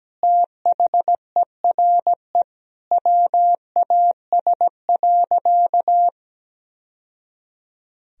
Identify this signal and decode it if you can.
Morse code, heard as a keyed tone, about 17 wpm, 700 Hz. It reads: THERE WAS.
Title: Sentences from Top 1000 Words - Review Part 3 - 17wpm